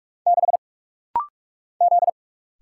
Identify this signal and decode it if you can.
Morse code, heard as a keyed tone, about 45 wpm, 700 Hz. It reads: 6 Z
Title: Single Letter-Number Alt - Rapid-Fire 45wpm